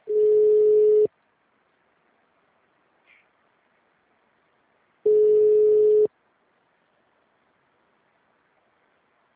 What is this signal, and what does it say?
Ringback tone (Europe, 425 Hz).